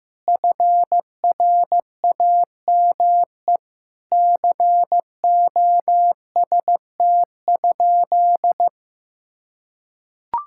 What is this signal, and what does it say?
Morse code, heard as a keyed tone, about 15 wpm, 700 Hz. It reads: FRAME COST?
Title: Sentences from Top 1000 Words - Review Part 6 - 15wpm